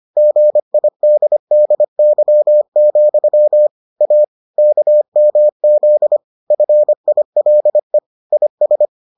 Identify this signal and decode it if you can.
Morse code, heard as a keyed tone, about 25 wpm, 600 Hz. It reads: GIDDY, A KMZ FILE IS